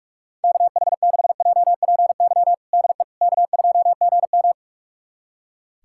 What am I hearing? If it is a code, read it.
Morse code, heard as a keyed tone, about 41 wpm, 690 Hz. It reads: KH6JPY DE K2GM
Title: USA Callsign DE Callsign 45wpm